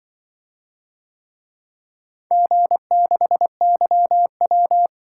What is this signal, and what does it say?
Morse code, heard as a keyed tone, about 24 wpm, 700 Hz. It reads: G6YW
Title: Speed-Racing - Four Letter-Number Combinations - 20wpm